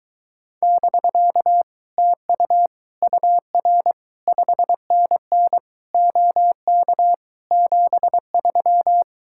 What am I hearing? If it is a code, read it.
Morse code, heard as a keyed tone, about 22 wpm, 700 Hz. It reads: <BK> TU UR 5NN OK 73